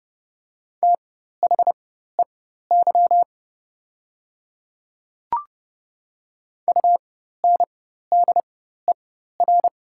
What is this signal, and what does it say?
Morse code, heard as a keyed tone, about 30 wpm, spaced as for 14 wpm, 690 Hz. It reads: THEY UNDER
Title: 4x Character Spacing - Top 300 Words - 30wpm